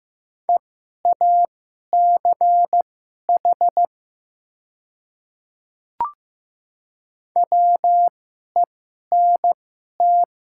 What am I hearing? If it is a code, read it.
Morse code, heard as a keyed tone, about 15 wpm, spaced as for 11 wpm, 700 Hz. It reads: EACH WENT
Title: 2x Character Spacing - Top 200 Words - 15wpm